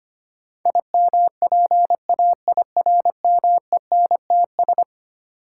Morse code, heard as a keyed tone, about 25 wpm, 700 Hz. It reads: IMPAIRMENTS
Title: Sets of 5 Words Encore - New Word Drill - 25wpm